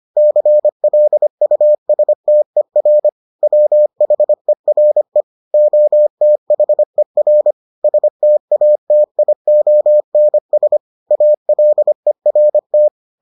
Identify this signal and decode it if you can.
Morse code, heard as a keyed tone, about 25 wpm, 600 Hz. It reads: CLUSTER WHERE OTHER STATIONS ALERT